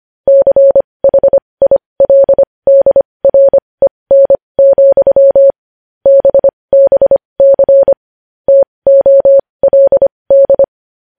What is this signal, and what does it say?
Morse code, heard as a keyed tone, about 25 wpm, spaced as for 20 wpm, 550 Hz. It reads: CHILDREN, BBC TOLD
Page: news-headlines-in-morse-code-at-20-wpm